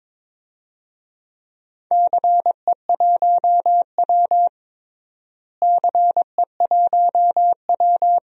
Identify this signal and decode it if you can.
Morse code, heard as a keyed tone, about 21 wpm, 700 Hz. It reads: CE1W CE1W
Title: DXCC-100 Call Signs - 4 Characters - Repeated 2x - 22wpm